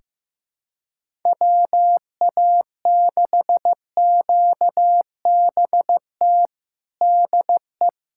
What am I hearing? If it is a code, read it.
Morse code, heard as a keyed tone, about 15 wpm, 700 Hz. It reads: WA6QBT DE
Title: USA Callsign DE Callsign 15wpm